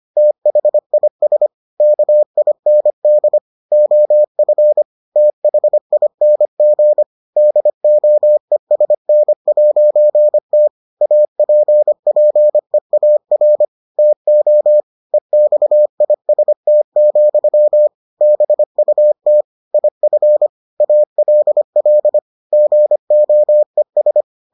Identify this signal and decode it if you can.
Morse code, heard as a keyed tone, about 25 wpm, 600 Hz. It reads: THIS KIND OF THING DOESN'T APPEAR TO EXIST, BUT IF ALL GOES